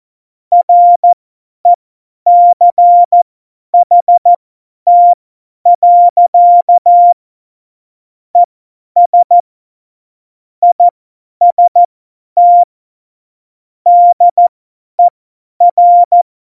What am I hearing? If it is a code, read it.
Morse code, heard as a keyed tone, about 14 wpm, spaced as for 10 wpm, 700 Hz. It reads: RECHT. ES IST DER
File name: Morsekurs / Witze / Witz 04. Juni 2019, 50 F70